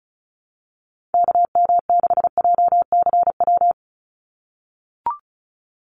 Morse code, heard as a keyed tone, about 34 wpm, 700 Hz. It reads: KM6JCW
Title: USA Call Signs 35wpm